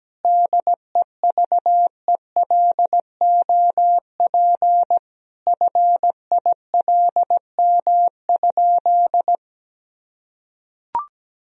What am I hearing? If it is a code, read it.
Morse code, heard as a keyed tone, about 17 wpm, 700 Hz. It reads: DEVELOP FILM?
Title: Sentences from Top 1000 Words - Review Part 3 - 17wpm